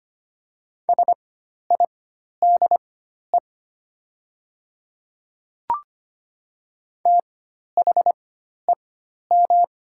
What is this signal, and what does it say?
Morse code, heard as a keyed tone, about 25 wpm, spaced as for 12 wpm, 700 Hz. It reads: SIDE THEM
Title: 4x Character Spacing - Top 100 Words - 25wpm